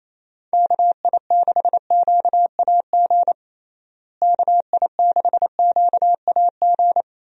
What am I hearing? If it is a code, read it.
Morse code, heard as a keyed tone, about 27 wpm, 700 Hz. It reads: KI6QAG KI6QAG
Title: USA Call Signs - 6 Characters - Repeated 3x - 28wpm